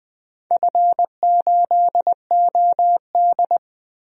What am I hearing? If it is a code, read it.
Morse code, heard as a keyed tone, about 20 wpm, 700 Hz. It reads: F8OD
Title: DXCC-100 Call Signs - 4 Characters - Repeated 2x - 20wpm